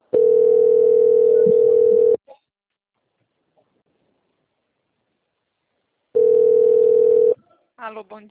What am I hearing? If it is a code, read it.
Ringback tone (North America), 440 + 480 Hz, with speech.
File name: out- - - - - .3013